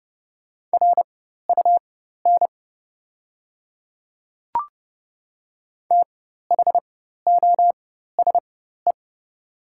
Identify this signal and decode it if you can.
Morse code, heard as a keyed tone, about 30 wpm, spaced as for 14 wpm, 700 Hz. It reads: RUN THOSE
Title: 4x Character Spacing - Top 500 Words - 30wpm